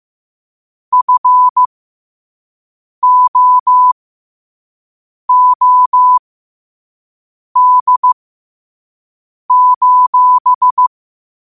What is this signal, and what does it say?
Morse code, heard as a keyed tone, about 15 wpm, spaced as for 5 wpm, 1000 Hz. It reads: FOOD:
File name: qotdfiles / 5wpm